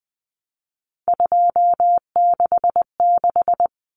Morse code, heard as a keyed tone, about 20 wpm, 700 Hz. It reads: 266
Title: Common Cut Numbers - 3-Digit 20wpm